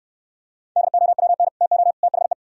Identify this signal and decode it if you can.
Morse code, heard as a keyed tone, about 67 wpm, 700 Hz. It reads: DOWN TO THE